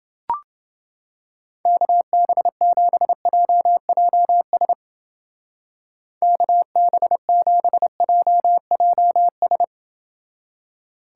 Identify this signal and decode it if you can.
Morse code, heard as a keyed tone, about 28 wpm, 700 Hz. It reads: KB7JJS KB7JJS